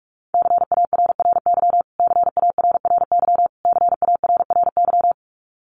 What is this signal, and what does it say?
Morse code, heard as a keyed tone, about 44 wpm, 700 Hz. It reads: CARRY CARRY CARRY
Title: Top 500 Words Repeated 45wpm